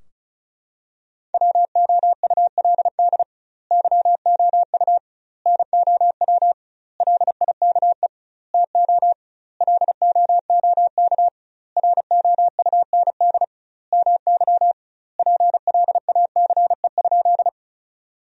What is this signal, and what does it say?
Morse code, heard as a keyed tone, about 36 wpm, 700 Hz. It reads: WOULD YOU NOW LIKE TO LOOK ROUND MY PLACE?
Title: 2x Word-Spacing - Sentences from Top 200 Words - 35wpm